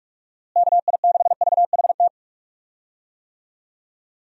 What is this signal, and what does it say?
Morse code, heard as a keyed tone, about 41 wpm, 700 Hz. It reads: KI6VHT
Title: USA Call Signs - 6 Characters - Repeated 3x - 45wpm